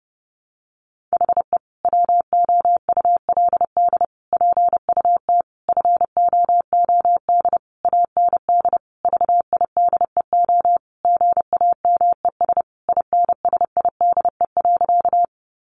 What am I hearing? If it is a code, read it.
Morse code, heard as a keyed tone, about 30 wpm, 700 Hz. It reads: HE WOULD PUT FOOD AND VIDEO GAMES INSIDE.
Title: Easy Phrases 30wpm